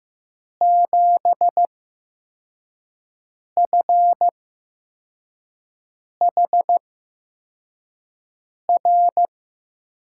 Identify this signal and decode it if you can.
Morse code, heard as a keyed tone, about 15 wpm, spaced as for 4 wpm, 700 Hz. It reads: 7FHR
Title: USA Call Signs - 8x Character Spacing - 15wpm